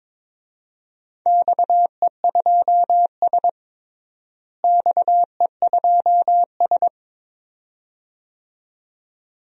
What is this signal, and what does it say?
Morse code, heard as a keyed tone, about 22 wpm, 700 Hz. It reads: XE2S XE2S
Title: DXCC-100 Call Signs - 4 Characters - Repeated 2x - 22wpm